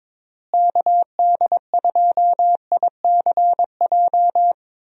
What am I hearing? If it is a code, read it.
Morse code, heard as a keyed tone, about 22 wpm, 700 Hz. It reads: KD2ICJ
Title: USA Call Signs - 6 Characters - Repeated 3x - 22wpm